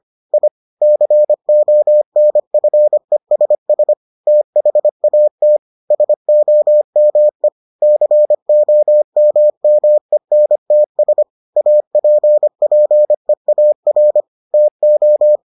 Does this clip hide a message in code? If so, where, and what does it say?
Yes: Morse code from 0.3 to 15.4 s, heard as a keyed tone, about 25 wpm, 600 Hz. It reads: I CONFESS THAT SOME COMMENTS APPEAR TO